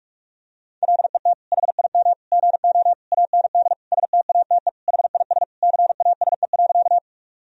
Morse code, heard as a keyed tone, about 46 wpm, 700 Hz. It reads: LET HIM GO AND STATE HIS CASE.
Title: Sentences from Top 1000 Words - Review Part 2 - 45wpm